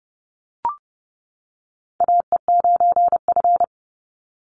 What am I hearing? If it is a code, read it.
Morse code, heard as a keyed tone, about 30 wpm, 700 Hz. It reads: AE9F
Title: NA SOTA Chaser Callsigns 30wpm